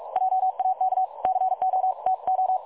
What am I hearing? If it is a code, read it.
Morse code, heard as a keyed tone, about 44 wpm, 750 Hz. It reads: JAK CO TO
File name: dlazaawansowanych / nagraniazpasma / websdr recording 3565.0kHz